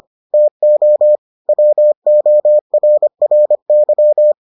Morse code, heard as a keyed tone, about 25 wpm, 600 Hz. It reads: TO WORRY